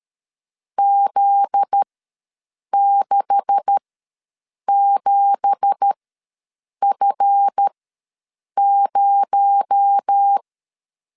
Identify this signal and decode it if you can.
Morse code, heard as a keyed tone, about 13 wpm, spaced as for 7 wpm, 800 Hz. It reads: Z67F0
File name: LEZIONI / 9 2867 ws 12wpm 3431